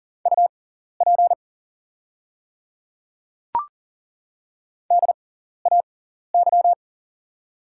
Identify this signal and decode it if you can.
Morse code, heard as a keyed tone, about 40 wpm, spaced as for 14 wpm, 700 Hz. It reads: UP DAY
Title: 6x Character Spacing - Top 200 Words - 40wpm